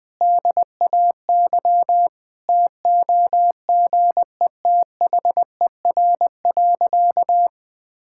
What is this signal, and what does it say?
Morse code, heard as a keyed tone, about 20 wpm, 700 Hz. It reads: DAY TOGETHER.
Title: Sentences from Top 1000 Words - Review Part 5 - 20wpm